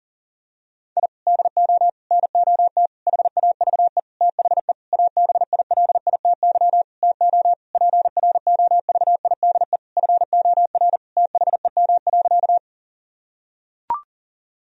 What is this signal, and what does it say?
Morse code, heard as a keyed tone, about 40 wpm, 700 Hz. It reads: I DO NOT HAVE THE ABILITY TO PROVIDE FOR THEM.
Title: Sentences from Top 1000 Words - Review Part 3 - 40wpm